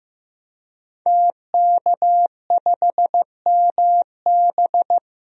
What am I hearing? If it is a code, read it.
Morse code, heard as a keyed tone, about 15 wpm, 700 Hz. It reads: TK5MB
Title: DXCC-100 Call Signs - 5 Characters - Repeated 3x - 15wpm